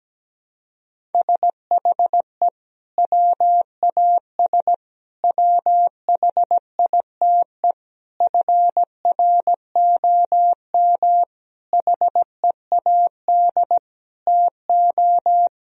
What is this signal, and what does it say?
Morse code, heard as a keyed tone, about 17 wpm, 700 Hz. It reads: SHE WAS WHITE FROM HEAD TO
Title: Sentences from Top 1000 Words - Review Part 3 - 17wpm